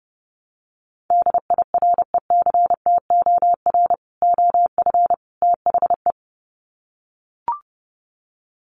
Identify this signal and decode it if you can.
Morse code, heard as a keyed tone, about 30 wpm, 700 Hz. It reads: DIRECTOR OF THE
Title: Sets of 3 Words 30wpm